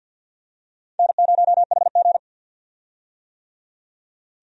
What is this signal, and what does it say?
Morse code, heard as a keyed tone, about 53 wpm, 700 Hz. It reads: N0HG
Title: USA Call Signs - 4 Characters - 50wpm